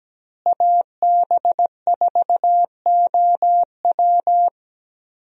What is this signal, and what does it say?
Morse code, heard as a keyed tone, about 17 wpm, 700 Hz. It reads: AB4OW
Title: USA Call Signs - 5 Characters - Repeated 3x - 17wpm